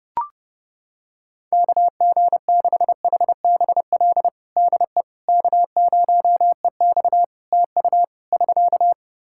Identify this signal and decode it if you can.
Morse code, heard as a keyed tone, about 30 wpm, 700 Hz. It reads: KG6HBL DE K0EX TU <SK>
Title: K1USN SST QSO Practice 30wpm